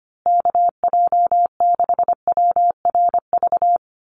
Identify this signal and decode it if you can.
Morse code, heard as a keyed tone, about 25 wpm, 700 Hz. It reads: KJ6WRV
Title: USA Call Signs Repeated 2x 25wpm